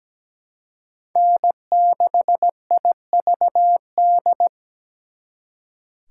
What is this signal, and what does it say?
Morse code, heard as a keyed tone, about 20 wpm, 700 Hz. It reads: N6IVD
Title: Speed Racing - USA 1x3 Call Signs 17wpm